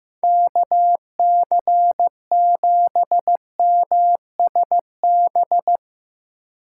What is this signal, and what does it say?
Morse code, heard as a keyed tone, about 16 wpm, 700 Hz. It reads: KC7MSB
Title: Speed Racing - USA 2x3 Call Signs 15wpm